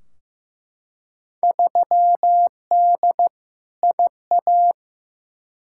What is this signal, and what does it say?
Morse code, heard as a keyed tone, about 15 wpm, 700 Hz. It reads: 3D IA
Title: Field Day Exchanges Encore - 15wpm